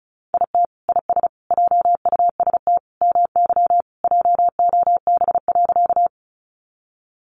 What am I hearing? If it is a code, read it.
Morse code, heard as a keyed tone, about 34 wpm, 700 Hz. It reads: IT IS JUST MY JOB.
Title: Sentences from Top 600 Words 35wpm